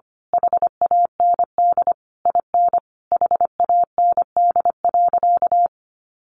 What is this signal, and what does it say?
Morse code, heard as a keyed tone, about 25 wpm, 700 Hz. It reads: HAND IN HAND.
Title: Sentences from Top 1200 Words 25wpm